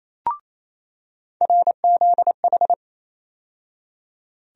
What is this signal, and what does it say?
Morse code, heard as a keyed tone, about 28 wpm, 700 Hz. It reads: RZH
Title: USA Call Sign Postfixes - 3 Letters 28wpm